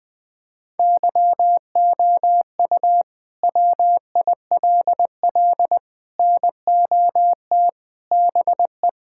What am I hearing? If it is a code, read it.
Morse code, heard as a keyed tone, about 20 wpm, 700 Hz. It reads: YOU WILL NOT BE
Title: Sets of 6 Words - 20wpm